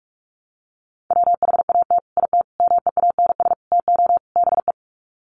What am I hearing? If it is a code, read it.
Morse code, heard as a keyed tone, about 44 wpm, 700 Hz. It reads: WHAT IT MEANS TO BE
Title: Sets of 5 Words 45wpm